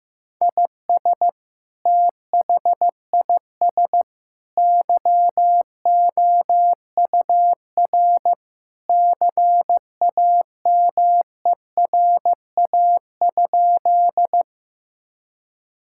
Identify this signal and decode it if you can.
Morse code, heard as a keyed tone, about 15 wpm, 700 Hz. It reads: IS THIS YOUR CAMERA?
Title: Sentences from Top 1000 Words - Review Part 5 - 15wpm